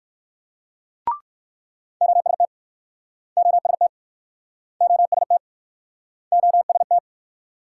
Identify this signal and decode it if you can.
Morse code, heard as a keyed tone, about 62 wpm, 700 Hz. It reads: OST OST OST OST